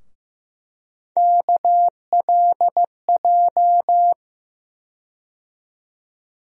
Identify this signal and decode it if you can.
Morse code, heard as a keyed tone, about 15 wpm, 700 Hz. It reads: KLJ
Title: USA Call Sign Postfixes - 3 Letters 15wpm